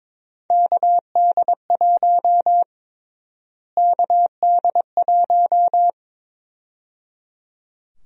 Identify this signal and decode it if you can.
Morse code, heard as a keyed tone, about 22 wpm, 700 Hz. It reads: KD1 KD1
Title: USA Call Sign Prefixes - Repeated 2x - 22wpm